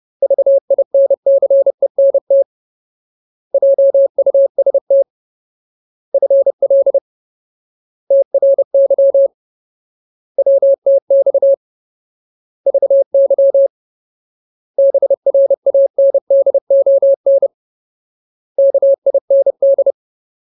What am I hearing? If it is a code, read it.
Morse code, heard as a keyed tone, about 30 wpm, 550 Hz. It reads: VINCENT JUST FL TRY WTX VY BRANDON KIND
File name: Ultimate Word List / 30WPM Ultimate